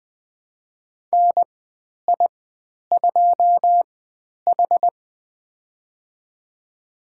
Morse code, heard as a keyed tone, about 20 wpm, spaced as for 10 wpm, 700 Hz. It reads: NI2H
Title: 095 Lesson 16g - Letter c - Call Signs 20wpm